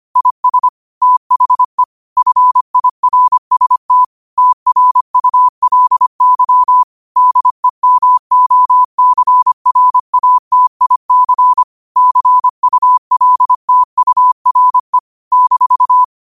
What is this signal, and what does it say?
Morse code, heard as a keyed tone, about 25 wpm, 1000 Hz. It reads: IS THE FIRST TRULY DEMOCRATIC CULTURE -
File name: qotdfiles / 25wpm